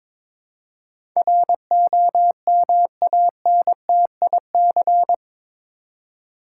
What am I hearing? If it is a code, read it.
Morse code, heard as a keyed tone, about 22 wpm, 700 Hz. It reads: ROMANTIC
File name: practice / 22wpm / Eight Letter Words - 22wpm